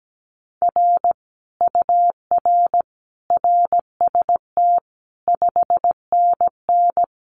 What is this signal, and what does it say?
Morse code, heard as a keyed tone, about 17 wpm, 700 Hz. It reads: R UR RST 5NN
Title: Common QSO Phrases Encore 17wpm